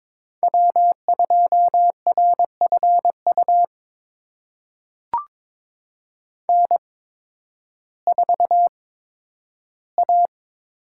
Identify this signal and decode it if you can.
Morse code, heard as a keyed tone, about 22 wpm, spaced as for 12 wpm, 700 Hz. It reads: W2RFU N4A
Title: USA Call Signs - 8x Character Spacing - 22wpm